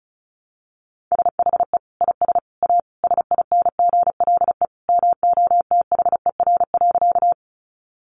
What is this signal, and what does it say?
Morse code, heard as a keyed tone, about 34 wpm, 700 Hz. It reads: SHE IS A SINGLE MOTHER.
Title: Sentences from Top 700 Words 35wpm